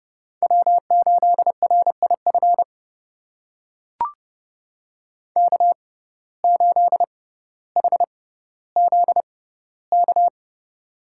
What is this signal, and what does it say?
Morse code, heard as a keyed tone, about 32 wpm, spaced as for 16 wpm, 700 Hz. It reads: W8RIF K8HZK
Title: USA Call Signs - 5 Characters - 6x Character Spacing - 30wpm